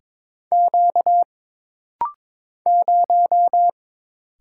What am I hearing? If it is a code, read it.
Morse code, heard as a keyed tone, about 22 wpm, 700 Hz. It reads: Q 0
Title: Single Letter-Number Alt - Rapid-Fire 22wpm